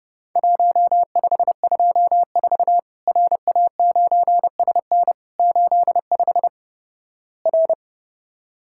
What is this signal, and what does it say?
Morse code, heard as a keyed tone, about 30 wpm, 700 Hz. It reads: 1524 RA9SN 85 R
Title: Worked All Europe DX Contest - QTC Exchange - 30wpm